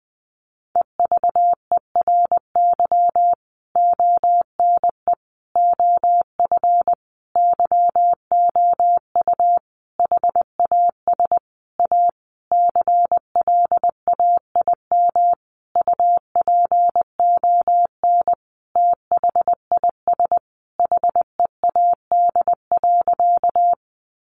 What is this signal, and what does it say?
Morse code, heard as a keyed tone, about 20 wpm, 700 Hz. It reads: EVERY ONE OF YOU HAS A CLAIM UPON THIS HEAD.
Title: Sentences from Top 1500 Words 20wpm